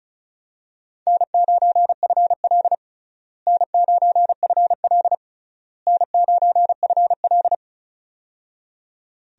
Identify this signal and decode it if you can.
Morse code, heard as a keyed tone, about 36 wpm, 700 Hz. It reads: N9FL N9FL N9FL
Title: USA Call Signs - 4 Characters - Repeated 3x - 35wpm